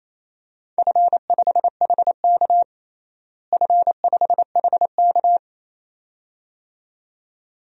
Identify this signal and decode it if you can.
Morse code, heard as a keyed tone, about 28 wpm, 700 Hz. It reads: F5HK F5HK
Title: DXCC-100 Call Signs - 4 Characters - Repeated 2x - 28wpm